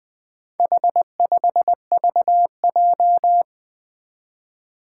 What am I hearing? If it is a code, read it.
Morse code, heard as a keyed tone, about 20 wpm, 700 Hz. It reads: H5VJ
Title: Four Challenging Letter-Number Combinations 20wpm